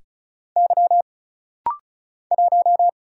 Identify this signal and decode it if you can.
Morse code, heard as a keyed tone, about 36 wpm, 700 Hz. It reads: Y 1
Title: Single Letter-Number Alt - Rapid-Fire 35wpm